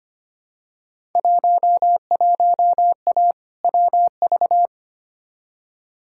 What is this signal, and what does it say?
Morse code, heard as a keyed tone, about 25 wpm, 700 Hz. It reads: 11A WV
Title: Field Day Exchanges Encore - 25wpm